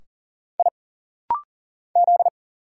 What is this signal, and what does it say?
Morse code, heard as a keyed tone, about 40 wpm, 700 Hz. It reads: I Z
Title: Single Letters - Rapid-Fire 40wpm